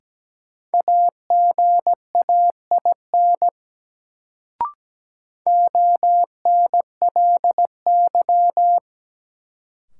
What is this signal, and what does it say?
Morse code, heard as a keyed tone, about 17 wpm, 700 Hz. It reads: AGAIN ONLY
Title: Sentences from Top 200 Words - New Word Drill - 17wpm